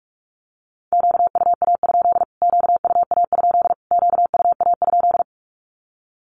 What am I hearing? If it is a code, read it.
Morse code, heard as a keyed tone, about 44 wpm, 700 Hz. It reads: QUA? QUA? QUA?
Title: Q-Codes Repeated 3x 45wpm